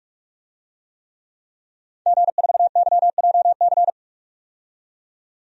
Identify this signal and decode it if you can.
Morse code, heard as a keyed tone, about 42 wpm, 700 Hz. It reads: G4YJC
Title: DXCC-100 Call Signs - 5 Characters - 45wpm